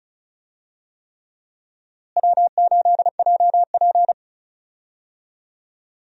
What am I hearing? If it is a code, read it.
Morse code, heard as a keyed tone, about 36 wpm, 700 Hz. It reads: W8JP
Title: USA Call Signs - 4 Characters - Repeated 3x - 35wpm